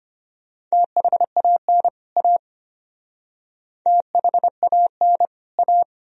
Morse code, heard as a keyed tone, about 32 wpm, 700 Hz. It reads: THAN A THAN A